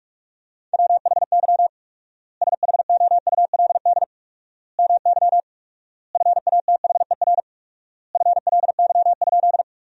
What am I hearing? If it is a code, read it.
Morse code, heard as a keyed tone, about 46 wpm, 700 Hz. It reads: WHY SHOULD MY FATHER FLY?
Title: 4x Word-Spacing - Sentences from Top 500 Words - 45wpm